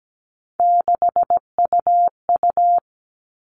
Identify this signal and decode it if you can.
Morse code, heard as a keyed tone, about 17 wpm, 700 Hz. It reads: 6UU
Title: All Cut Numbers - 3-Digit 17wpm